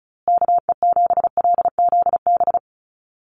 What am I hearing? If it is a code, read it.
Morse code, heard as a keyed tone, about 34 wpm, 700 Hz. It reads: KE7LZB
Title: USA Call Signs Repeated 2x 35wpm